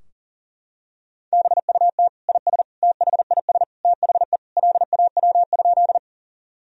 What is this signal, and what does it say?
Morse code, heard as a keyed tone, about 40 wpm, 700 Hz. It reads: BUT IS THIS THE LAW?
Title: Sentences from Top 1000 Words - Review Part 3 - 40wpm